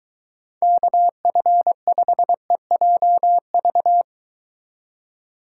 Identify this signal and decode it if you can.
Morse code, heard as a keyed tone, about 23 wpm, 700 Hz. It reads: KF5EJV